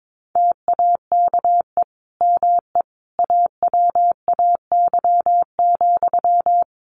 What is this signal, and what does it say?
Morse code, heard as a keyed tone, about 22 wpm, 700 Hz. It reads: TAKE ME AWAY,
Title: Sentences from Top 1400 Words 22wpm